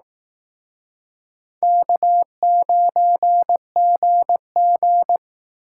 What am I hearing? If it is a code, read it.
Morse code, heard as a keyed tone, about 18 wpm, 700 Hz. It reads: K9GG
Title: Speed Racing - USA 1x2 Call Signs 15wpm